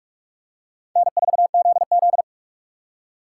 Morse code, heard as a keyed tone, about 46 wpm, 700 Hz. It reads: N4ZZ
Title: USA Call Signs - 4 Characters - 2x Character Spacing - 45wpm